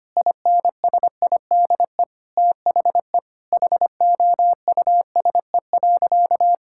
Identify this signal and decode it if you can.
Morse code, heard as a keyed tone, about 25 wpm, 700 Hz. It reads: INSIDE THE HOUSE.
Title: Sentences from Top 1000 Words - Review Part 6 - 25wpm